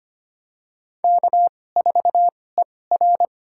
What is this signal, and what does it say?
Morse code, heard as a keyed tone, about 25 wpm, spaced as for 18 wpm, 700 Hz. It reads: K4ER
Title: USA Call Signs - 4 Characters - 2x Character Spacing - 25wpm